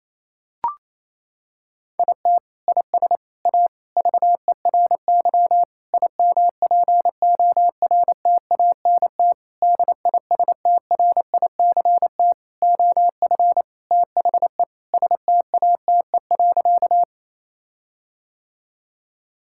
Morse code, heard as a keyed tone, about 28 wpm, 700 Hz. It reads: IT IS A VERY IMPORTANT DISTRICT OF THE STATE.
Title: Sentences from Top 1600 Words 28wpm